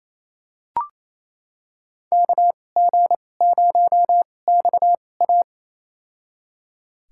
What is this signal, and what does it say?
Morse code, heard as a keyed tone, about 28 wpm, spaced as for 20 wpm, 700 Hz. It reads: KG0XA
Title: USA Call Signs - 5 Characters - 2x Character Spacing - 28wpm